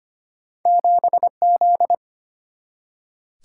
Morse code, heard as a keyed tone, about 32 wpm, 700 Hz. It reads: 7Z